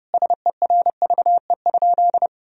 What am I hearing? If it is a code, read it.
Morse code, heard as a keyed tone, about 32 wpm, 700 Hz. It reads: SERVE?
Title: Sentences from Top 1000 Words - Review Part 2 - 30wpm